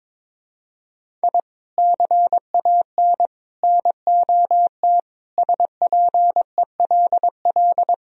Morse code, heard as a keyed tone, about 22 wpm, 700 Hz. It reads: I CAN NOT SPELL
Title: Sentences from Top 1000 Words - Review Part 1 - 22wpm